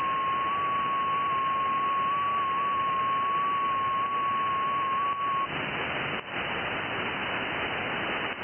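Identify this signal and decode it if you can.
FT8.
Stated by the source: FT8 signal received on an IC-7100 radio.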